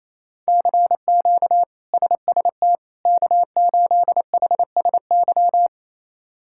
Morse code, heard as a keyed tone, about 28 wpm, 700 Hz. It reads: CQ SST K8HSY
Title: K1USN SST QSO Practice - 28wpm